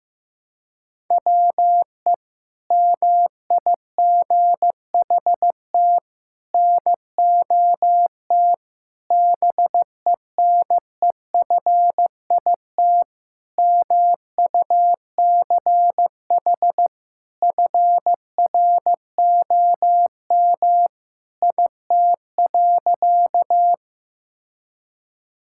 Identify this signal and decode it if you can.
Morse code, heard as a keyed tone, about 15 wpm, 700 Hz. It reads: WE MIGHT NOT BENEFIT MUCH FROM IT.
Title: Sentences from Top 1000 Words - Review Part 6 - 15wpm